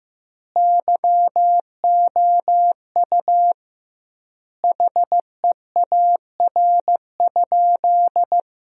Morse code, heard as a keyed tone, about 15 wpm, 700 Hz. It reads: YOU HEAR?
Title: 2x Word-Spacing - Sentences from Top 300 Words - 15wpm